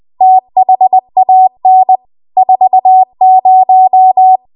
Morse code, heard as a keyed tone, about 20 wpm, 750 Hz. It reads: THAN 40
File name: Morse / 20WPM